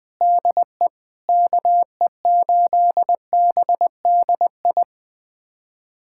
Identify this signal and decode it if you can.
Morse code, heard as a keyed tone, about 20 wpm, 700 Hz. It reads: DE KE8BDI
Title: USA Callsign DE Callsign 20wpm